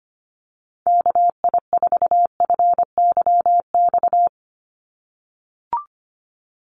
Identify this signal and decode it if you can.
Morse code, heard as a keyed tone, about 25 wpm, 700 Hz. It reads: KI4FYX
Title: USA Call Signs 25wpm